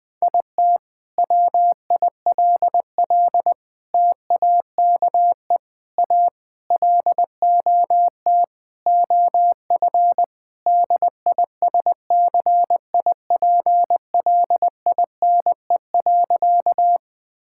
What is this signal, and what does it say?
Morse code, heard as a keyed tone, about 21 wpm, 700 Hz. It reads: IT WILL TAKE A LOT OF DISCIPLINE.
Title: Sentences from Top 1000 Words - Review Part 6 - 20wpm